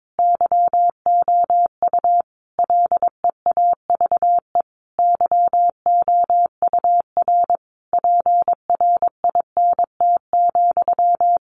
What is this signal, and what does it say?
Morse code, heard as a keyed tone, about 22 wpm, 700 Hz. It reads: YOU LEAVE YOUR PRINT,
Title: A Sound Of Thunder 22wpm